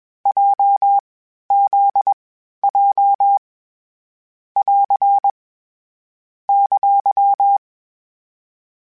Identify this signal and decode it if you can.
Morse code, heard as a keyed tone, about 21 wpm, 800 Hz. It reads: J Z J <AR> !
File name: Basic Sets / Basic Letter Groups Collective7 / JZ17@20